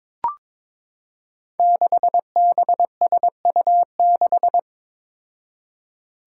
Morse code, heard as a keyed tone, about 22 wpm, 700 Hz. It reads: 6BSU6
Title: Five Challenging Letter-Number Combinations 22wpm